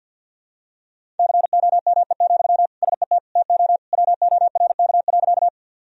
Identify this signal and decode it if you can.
Morse code, heard as a keyed tone, about 53 wpm, 700 Hz. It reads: COME, SET TO WORK.
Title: Sentences from Top 1000 Words - Review Part 1 - 50wpm